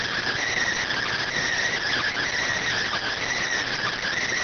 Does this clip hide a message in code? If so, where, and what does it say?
Yes: Emergency Alert System (SAME) tones from 0.9 to 3.4 s, heard as harsh buzzing bursts.